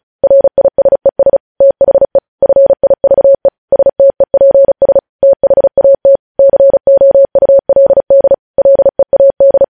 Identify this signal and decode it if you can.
Morse code, heard as a keyed tone, about 36 wpm, 550 Hz. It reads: RISES THE FIVE STEPS THAT COULD LEAD
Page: News headlines in Morse code at 35 wpm